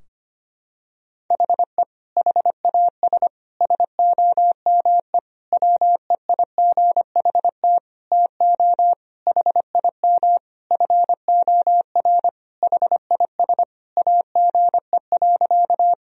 Morse code, heard as a keyed tone, about 25 wpm, 700 Hz. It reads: HE HAS SOME WEIGHT TO HIM FOR HIS AGE.
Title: Sentences from Top 1000 Words - Review Part 3 - 25wpm